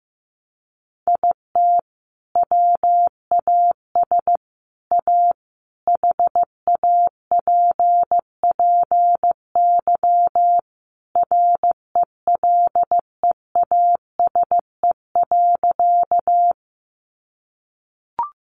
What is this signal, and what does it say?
Morse code, heard as a keyed tone, about 15 wpm, 700 Hz. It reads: IT WAS A HAPPY RELEASE.
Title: Sentences from Top 1100 Words 15wpm